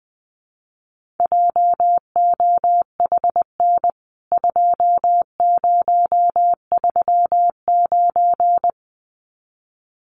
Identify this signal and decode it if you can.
Morse code, heard as a keyed tone, about 20 wpm, 700 Hz. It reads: JOHN 2039
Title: CWT Exchange 20wpm